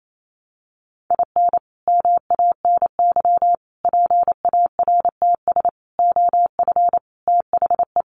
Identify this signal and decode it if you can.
Morse code, heard as a keyed tone, about 28 wpm, 700 Hz. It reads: IN MANY PARTS OF THE
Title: Sets of 5 Words Encore 28wpm